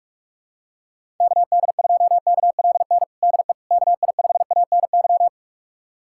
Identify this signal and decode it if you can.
Morse code, heard as a keyed tone, about 48 wpm, 700 Hz. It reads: KD2KLN DE KI5ANY
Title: USA Callsign DE Callsign - 6 Characters - 45wpm